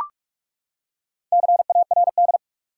Morse code, heard as a keyed tone, about 43 wpm, 700 Hz. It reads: CARD
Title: Four Letter Words - 45wpm